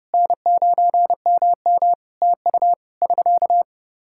Morse code, heard as a keyed tone, about 30 wpm, 700 Hz. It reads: N9MM TU <SK>
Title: K1USN SST QSO Practice 30wpm